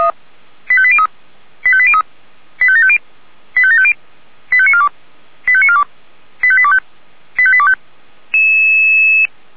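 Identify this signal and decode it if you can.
Five-tone selective call (ZVEI), heard as quick runs of five tones, tones of about 70 ms. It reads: 85803 85803 85750 85750 85942 85942 85825 85825
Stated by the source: ZVEI five-tone selective call received on an SDR.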